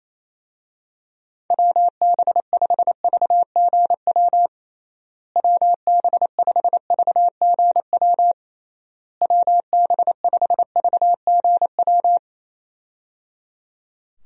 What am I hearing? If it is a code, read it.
Morse code, heard as a keyed tone, about 28 wpm, 700 Hz. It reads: WB5VGW WB5VGW WB5VGW
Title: USA Call Signs - 6 Characters - Repeated 3x - 28wpm